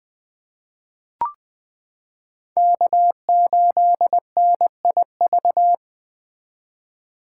Morse code, heard as a keyed tone, about 21 wpm, 700 Hz. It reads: K8NIV
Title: 2023 POTA Hunter Call Signs - 20wpm